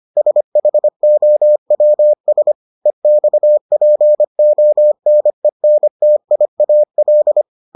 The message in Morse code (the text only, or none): SHOWS EXPONENTIAL